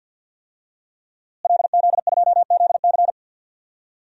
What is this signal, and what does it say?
Morse code, heard as a keyed tone, about 52 wpm, 700 Hz. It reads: LZ2ZC
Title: DXCC-100 Call Signs - 5 Characters - Repeated 2x - 50wpm